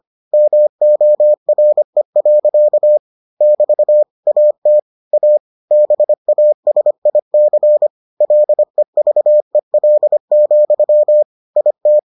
Morse code, heard as a keyed tone, about 25 wpm, 600 Hz. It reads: MORE. <BT> AT A BASIC LEVEL, IT